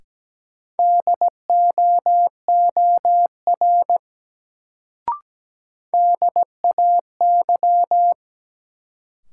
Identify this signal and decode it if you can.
Morse code, heard as a keyed tone, about 17 wpm, 700 Hz. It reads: DOOR DAY
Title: Sets of 2 Words - New Word Drill - 17wpm